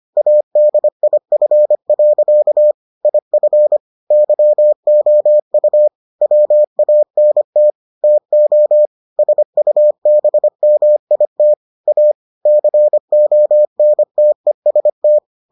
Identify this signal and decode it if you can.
Morse code, heard as a keyed tone, about 25 wpm, 600 Hz. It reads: ADIF. IF YOU WANT TO SUBMIT A CONTEST